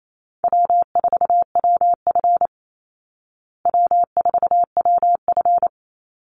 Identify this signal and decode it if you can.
Morse code, heard as a keyed tone, about 28 wpm, 700 Hz. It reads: W4WF W4WF
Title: Easy Call Signs Repeated 3x - 28wpm